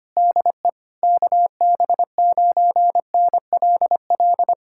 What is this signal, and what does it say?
Morse code, heard as a keyed tone, about 24 wpm, 700 Hz. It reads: DE KB9NLL
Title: Speed-Racing Common QSO Phrases - 25wpm